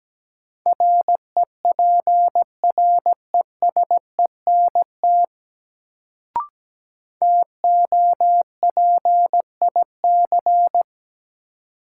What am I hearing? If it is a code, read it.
Morse code, heard as a keyed tone, about 17 wpm, 700 Hz. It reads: REPRESENT TOPIC
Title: Sentences from Top 1200 Words - New Word Drill Alt - 17wpm